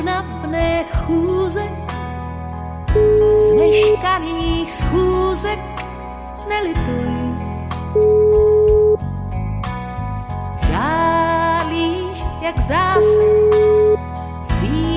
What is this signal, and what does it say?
Ringback tone (Europe, 425 Hz).